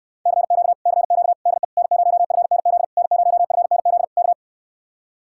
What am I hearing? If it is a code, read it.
Morse code, heard as a keyed tone, about 69 wpm, 700 Hz. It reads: CQ CQ DE N0FNZ N0FNZ K